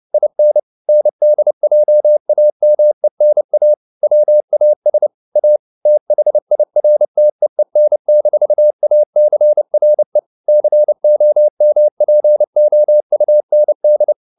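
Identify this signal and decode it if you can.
Morse code, heard as a keyed tone, about 30 wpm, 600 Hz. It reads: IN NDJAMENA WAS A THIRTEEN-ACRE COMPOUND
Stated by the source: Morse code at 29 wpm, sending Ken Follett's novel Never in English, from CW med Gnister.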